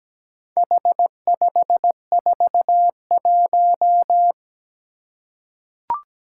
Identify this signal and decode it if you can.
Morse code, heard as a keyed tone, about 17 wpm, 700 Hz. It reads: H541
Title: Four Challenging Letter-Number Combinations 17wpm